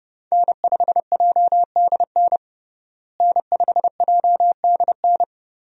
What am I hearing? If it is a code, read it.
Morse code, heard as a keyed tone, about 30 wpm, 700 Hz. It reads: N5JDN N5JDN
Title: USA Call Signs - 5 Characters - Repeated 3x - 30wpm